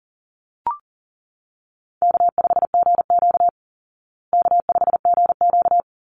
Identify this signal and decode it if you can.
Morse code, heard as a keyed tone, about 40 wpm, 700 Hz. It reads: K5GQ K5GQ
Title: Easy Call Signs Repeated 2x - 40wpm